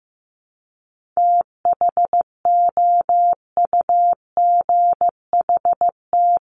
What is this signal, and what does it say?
Morse code, heard as a keyed tone, about 15 wpm, 700 Hz. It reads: THOUGHT
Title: Top 500 Words 15wpm